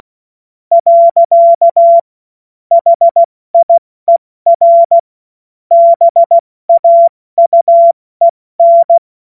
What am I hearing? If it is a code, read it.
Morse code, heard as a keyed tone, about 16 wpm, 680 Hz. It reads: . HIER BAUEN
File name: Morsekurs / Witze / Witz 14.November 2023 70 F80